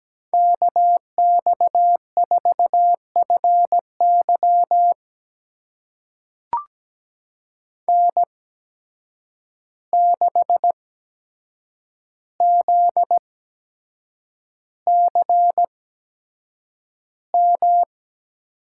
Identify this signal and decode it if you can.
Morse code, heard as a keyed tone, about 17 wpm, spaced as for 7 wpm, 700 Hz. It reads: KX4FY N6ZCM
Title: USA Call Signs - 5 Characters - 8x Character Spacing - 17wpm